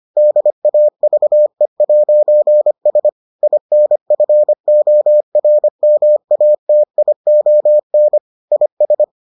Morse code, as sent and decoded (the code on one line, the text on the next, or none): -.. .- ...- . .----. ... / .. -. ..-. --- .-. -- .- - .. --- -. / .. ...
DAVE'S INFORMATION IS